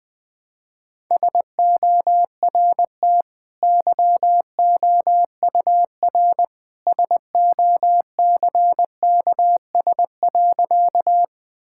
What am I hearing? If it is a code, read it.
Morse code, heard as a keyed tone, about 21 wpm, 700 Hz. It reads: SORT YOUR SOCKS.
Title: Easy Sentences with Hints - 20wpm